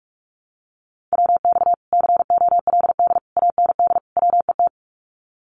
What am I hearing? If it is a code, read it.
Morse code, heard as a keyed tone, about 46 wpm, 700 Hz. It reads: WX COLD AND WET
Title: Common QSO Phrases Encore 45wpm